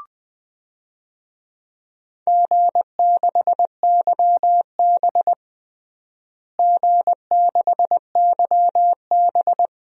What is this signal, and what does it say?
Morse code, heard as a keyed tone, about 20 wpm, 700 Hz. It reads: G6YB G6YB
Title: DXCC-100 Call Signs - 4 Characters - Repeated 3x - 20wpm